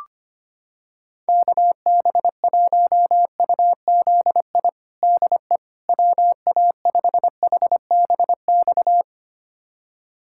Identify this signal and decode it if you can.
Morse code, heard as a keyed tone, about 25 wpm, 700 Hz. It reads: KB1UZI DE WA5HBX
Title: USA Callsign DE Callsign 25wpm